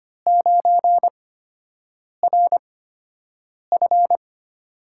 Morse code, heard as a keyed tone, about 25 wpm, spaced as for 7 wpm, 700 Hz. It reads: 9RF
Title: USA Call Signs - 5 Characters - 8x Character Spacing - 25wpm